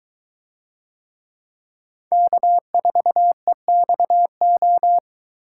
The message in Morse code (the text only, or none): K4EXO